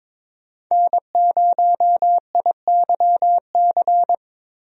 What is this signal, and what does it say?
Morse code, heard as a keyed tone, about 22 wpm, 700 Hz. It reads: N0IYC
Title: USA Call Signs - 5 Characters - Repeated 3x - 22wpm